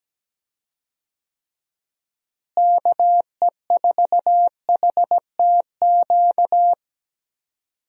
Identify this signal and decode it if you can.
Morse code, heard as a keyed tone, about 17 wpm, 700 Hz. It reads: KE4HTQ
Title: USA Call Signs - 6 Characters - Repeated 2x - 17wpm